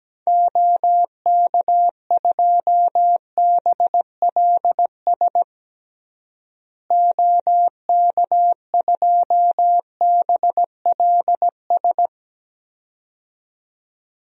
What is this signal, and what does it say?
Morse code, heard as a keyed tone, about 17 wpm, 700 Hz. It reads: OK2BLS OK2BLS
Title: DXCC-100 Call Signs - 6 Characters - Repeated 2x - 17wpm